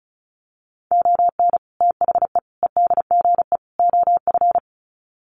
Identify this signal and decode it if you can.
Morse code, heard as a keyed tone, about 34 wpm, 700 Hz. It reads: ON THE EDGE OF
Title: Sets of 4 Words 35wpm